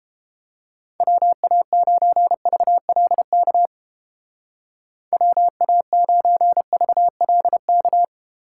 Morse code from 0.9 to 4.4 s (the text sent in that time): WA9VLK